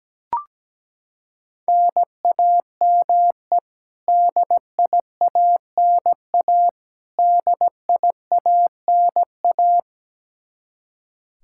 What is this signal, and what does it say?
Morse code, heard as a keyed tone, about 17 wpm, 700 Hz. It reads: NAME DIANA DIANA
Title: Practice QSO - Exchange 1 - Beginning - 17wpm